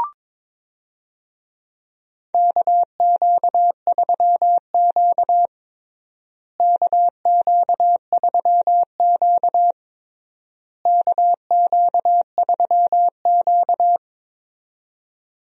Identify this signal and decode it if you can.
Morse code, heard as a keyed tone, about 22 wpm, 700 Hz. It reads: KQ3Q KQ3Q KQ3Q
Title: USA Call Signs - 4 Characters - Repeated 3x - 22wpm